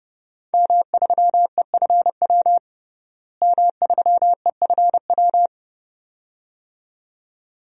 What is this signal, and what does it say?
Morse code, heard as a keyed tone, about 32 wpm, 700 Hz. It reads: M3EFW M3EFW
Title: DXCC-100 Call Signs - 5 Characters - Repeated 2x - 30wpm